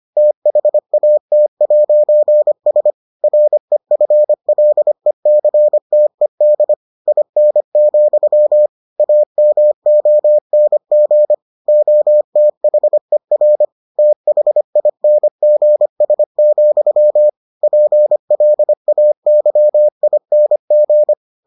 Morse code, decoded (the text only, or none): THAT'S REFLECTED IN, AMONG OTHER THINGS, PLAYING